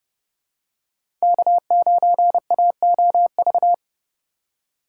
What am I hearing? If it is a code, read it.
Morse code, heard as a keyed tone, about 32 wpm, 700 Hz. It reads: K9AOV
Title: USA Call Signs - 5 Characters - 4x Character Spacing - 30wpm